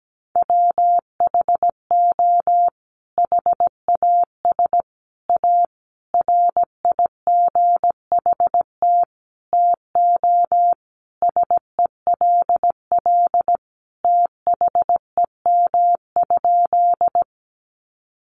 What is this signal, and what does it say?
Morse code, heard as a keyed tone, about 17 wpm, 700 Hz. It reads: WHO HAS A RIGHT TO SELL THEM?
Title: Sentences from Top 1100 Words 17wpm